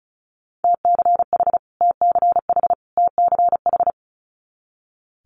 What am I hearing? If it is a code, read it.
Morse code, heard as a keyed tone, about 36 wpm, 700 Hz. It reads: TCH TCH TCH
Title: Multi-letter Phonemes Repeated 3x 35wpm